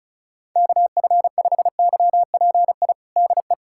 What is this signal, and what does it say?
Morse code, heard as a keyed tone, about 34 wpm, 700 Hz. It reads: KF5YPI DE
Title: USA Callsign DE Callsign 35wpm